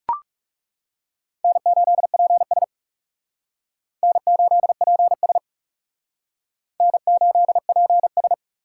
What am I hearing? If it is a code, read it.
Morse code, heard as a keyed tone, about 37 wpm, 700 Hz. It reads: N8PS N8PS N8PS
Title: Speed Racing - USA 1x2 Call Signs 30wpm